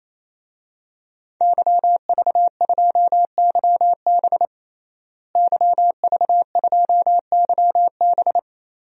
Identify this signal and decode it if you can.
Morse code, heard as a keyed tone, about 28 wpm, 700 Hz. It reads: YV2YB YV2YB
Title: DXCC-100 Call Signs - 5 Characters - Repeated 3x - 28wpm